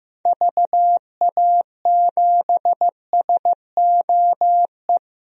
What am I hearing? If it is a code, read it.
Morse code, heard as a keyed tone, about 15 wpm, 700 Hz. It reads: VA7SOE
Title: DXCC-100 Call Signs - 6 Characters - Repeated 2x - 15wpm